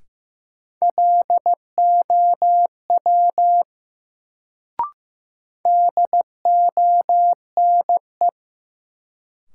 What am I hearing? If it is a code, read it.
Morse code, heard as a keyed tone, about 15 wpm, 700 Hz. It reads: LOW DONE
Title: Sentences from Top 500 Words - New Word Drill - 15wpm